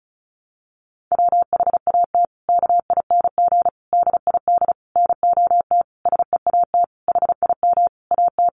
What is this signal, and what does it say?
Morse code, heard as a keyed tone, about 34 wpm, 700 Hz. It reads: WHAT KING DID NOT SEAT HIM AT
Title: Sentences from Top 1500 Words 35wpm